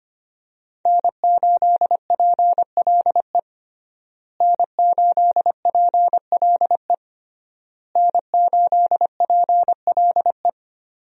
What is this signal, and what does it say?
Morse code, heard as a keyed tone, about 25 wpm, 700 Hz. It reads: N8PLE N8PLE N8PLE
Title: USA Call Signs - 5 Characters - Repeated 3x - 25wpm